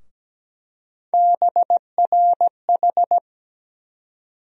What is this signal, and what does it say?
Morse code, heard as a keyed tone, about 17 wpm, 700 Hz. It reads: BRH
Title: USA Call Sign Postfixes - 3 Letters 17wpm